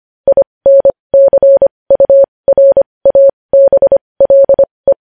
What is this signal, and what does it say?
Morse code, heard as a keyed tone, about 25 wpm, spaced as for 20 wpm, 550 Hz. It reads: INCURABLE
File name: podcast / News headlines-1x-25x20